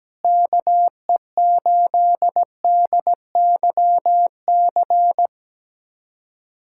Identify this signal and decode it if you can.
Morse code, heard as a keyed tone, about 17 wpm, 700 Hz. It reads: KE8DYC
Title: USA Callsign DE Callsign - 6 Characters - 17wpm